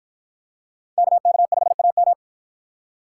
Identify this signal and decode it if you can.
Morse code, heard as a keyed tone, about 47 wpm, 700 Hz. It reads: KK5AM